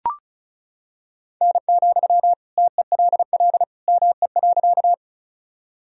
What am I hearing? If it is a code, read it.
Morse code, heard as a keyed tone, about 36 wpm, 700 Hz. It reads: N, TELL ME.
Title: Sentences from Top 200 Words and Abbreviations 35wpm